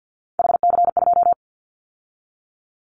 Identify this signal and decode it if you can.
Morse code, heard as a keyed tone, about 50 wpm, 700 Hz. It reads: HC2
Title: Callsign Prefixes for Top 100 DXCC - 50wpm